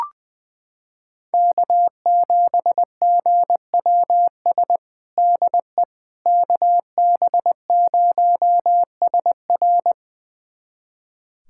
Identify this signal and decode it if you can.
Morse code, heard as a keyed tone, about 21 wpm, 700 Hz. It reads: K7GWS DE KB0SR
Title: USA Callsign DE Callsign - 5 Characters - 20wpm